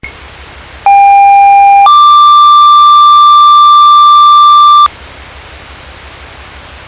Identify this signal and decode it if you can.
Two-tone page, heard as a one-second tone, then a three-second tone, about 800 then 1180 Hz. It is heard from 0.9 to 4.9 s.